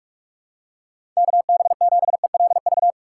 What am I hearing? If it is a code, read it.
Morse code, heard as a keyed tone, about 46 wpm, 700 Hz. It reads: KB7ELV
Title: USA Call Signs - 6 Characters - 6x Character Spacing - 45wpm